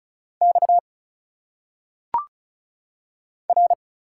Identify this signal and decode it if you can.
Morse code, heard as a keyed tone, about 35 wpm, 700 Hz. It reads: X R
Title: Single Letters - Phonetic Pronunciation 35wpm